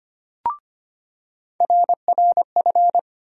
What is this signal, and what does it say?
Morse code, heard as a keyed tone, about 25 wpm, 700 Hz. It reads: RRF
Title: USA Call Sign Postfixes - 3 Letters 25wpm